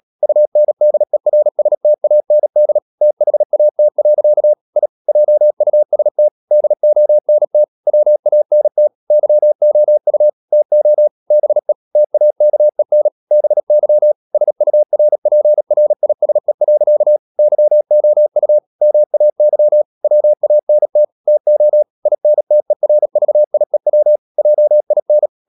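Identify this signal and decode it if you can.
Morse code, heard as a keyed tone, about 37 wpm, 600 Hz. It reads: UNDERSTAND THAT. I JUST DONT WANT YOU TO BE TAKEN BY SURPRISE. YOU MAY WANT TO INTERVIEW JIN
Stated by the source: Morse code practice at 35–39 wpm sending English text of Ken Follett's novel Never, from CW med Gnister.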